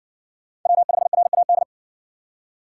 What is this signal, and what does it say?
Morse code, heard as a keyed tone, about 59 wpm, 700 Hz. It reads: W5RAD